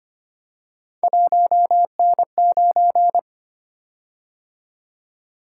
Morse code, heard as a keyed tone, about 25 wpm, 700 Hz. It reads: 1N9
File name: Three Letter-Number Combinations - 25wpm